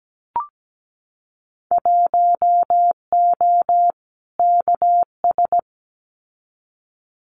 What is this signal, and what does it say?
Morse code, heard as a keyed tone, about 17 wpm, 700 Hz. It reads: 1O KS
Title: Winter Field Day Exchanges 17wpm